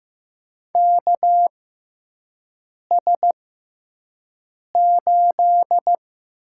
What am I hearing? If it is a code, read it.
Morse code, heard as a keyed tone, about 15 wpm, spaced as for 5 wpm, 700 Hz. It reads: KS8
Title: USA Call Signs - 4 Characters - 6x Character Spacing - 15wpm